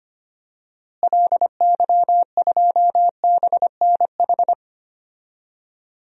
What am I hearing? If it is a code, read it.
Morse code, heard as a keyed tone, about 25 wpm, 700 Hz. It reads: LY2BNH
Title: DXCC-100 Call Signs - 6 Characters - Repeated 3x - 25wpm